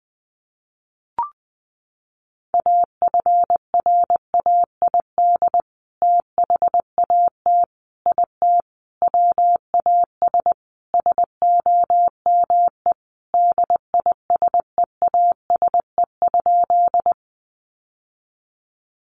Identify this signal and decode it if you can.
Morse code, heard as a keyed tone, about 20 wpm, 700 Hz. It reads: AFRAID THAT IT WAS SOME DISEASE?
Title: Sentences from Top 1500 Words 20wpm